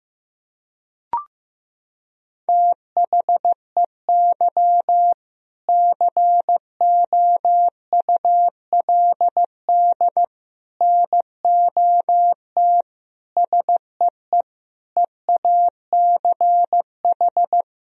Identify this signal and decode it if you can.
Morse code, heard as a keyed tone, about 15 wpm, 700 Hz. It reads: THEY COULD NOT SEE EACH
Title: Sentences from Top 1000 Words - Review Part 1 - 15wpm